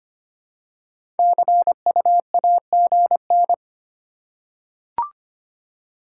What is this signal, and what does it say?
Morse code, heard as a keyed tone, about 25 wpm, 700 Hz. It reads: CUAGN
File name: Top 20 QSO Elements - 3x Character Spacing - 25wpm